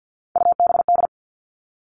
Morse code, heard as a keyed tone, about 50 wpm, 700 Hz. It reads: UBD
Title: All Cut Numbers - 3-Digit 50wpm